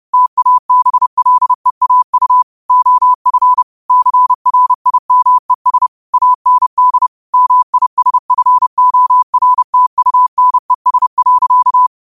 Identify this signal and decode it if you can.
Morse code, heard as a keyed tone, about 30 wpm, 1000 Hz. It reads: TABLEAU OF CRIMES AND MISFORTUNES.
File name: qotdfiles / 30wpm